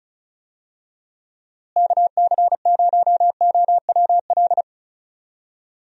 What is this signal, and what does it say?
Morse code, heard as a keyed tone, about 34 wpm, 700 Hz. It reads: KC0OWL
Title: USA Call Signs - 6 Characters - Repeated 3x - 35wpm